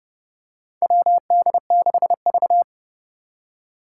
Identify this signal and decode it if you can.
Morse code, heard as a keyed tone, about 30 wpm, 700 Hz. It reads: WD6V
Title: Four Challenging Letter-Number Combinations 30wpm